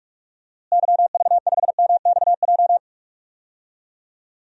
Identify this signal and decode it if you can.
Morse code, heard as a keyed tone, about 46 wpm, 700 Hz. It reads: YV5MXJ
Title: DXCC-100 Call Signs - 6 Characters - Repeated 2x - 45wpm